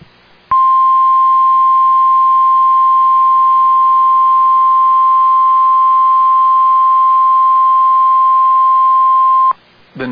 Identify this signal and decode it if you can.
Emergency Alert System (SAME) tones (the 1050 Hz attention tone). It is heard from 0.5 to 9.5 s.